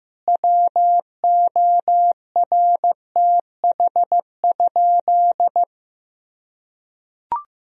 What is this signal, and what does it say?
Morse code, heard as a keyed tone, about 15 wpm, 700 Hz. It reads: WORTH?
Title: Sentences from Top 1000 Words - Review Part 1 - 15wpm